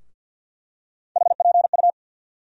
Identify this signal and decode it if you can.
Morse code, heard as a keyed tone, about 52 wpm, 700 Hz. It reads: HPU
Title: USA Call Sign Postfixes - 3 Letters 50wpm